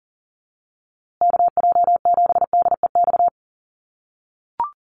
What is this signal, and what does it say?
Morse code, heard as a keyed tone, about 40 wpm, 700 Hz. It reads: KJ7DEX
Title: USA Call Signs Repeated 3x 40wpm